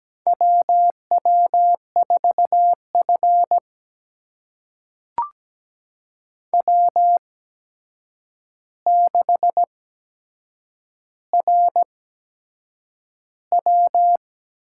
Morse code, heard as a keyed tone, about 17 wpm, spaced as for 7 wpm, 700 Hz. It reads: WW4F W6RW
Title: USA Call Signs - 4 Characters - 8x Character Spacing - 17wpm